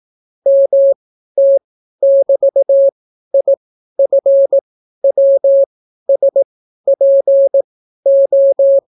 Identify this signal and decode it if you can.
Morse code, heard as a keyed tone, about 18 wpm, spaced as for 12 wpm, 550 Hz. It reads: MT<BT>IFWSPO